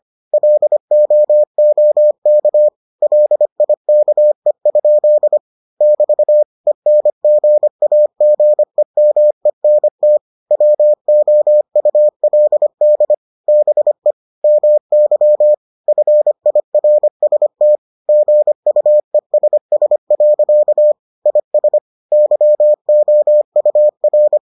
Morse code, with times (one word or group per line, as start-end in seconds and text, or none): 0.3-2.7 LOOK
3.0-5.4 LIKE?
5.8-6.4 <BT>
6.7-10.2 ENGAGEMENT
10.5-13.1 WOULD
13.5-14.1 BE
14.4-15.5 MY
15.9-17.8 FIRST
18.1-20.9 GUESS.
21.3-21.8 IS
22.1-24.4 YOUR